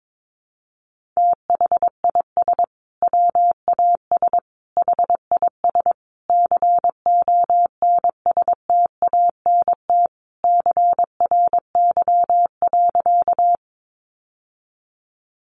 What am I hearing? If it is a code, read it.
Morse code, heard as a keyed tone, about 22 wpm, 700 Hz. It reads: THIS WAS HIS CONSTANT CRY.
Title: Sentences from Top 1500 Words 22wpm